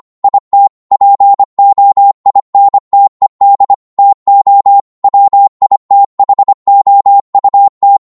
Morse code, heard as a keyed tone, about 25 wpm, 800 Hz. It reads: IT POINTED TO WITHOUT
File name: 25wpm